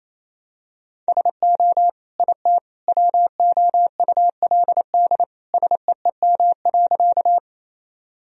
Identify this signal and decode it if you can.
Morse code, heard as a keyed tone, about 28 wpm, 700 Hz. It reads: SO IT WOULD SEEM.
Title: Sentences from Top 1000 Words - Review Part 1 - 28wpm